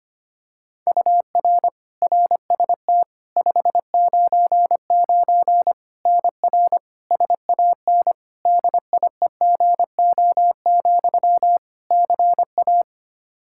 Morse code, heard as a keyed tone, about 25 wpm, 700 Hz. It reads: UR RST 599 NR SAN DIEGO, CA
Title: Practice QSO - Exchange 1 - Beginning - 25wpm